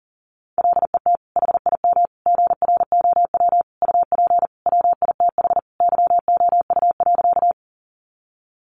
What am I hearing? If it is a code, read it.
Morse code, heard as a keyed tone, about 40 wpm, 700 Hz. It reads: LET HIM GROW UP WITH YOU.
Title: Sentences from Top 300 Words 40wpm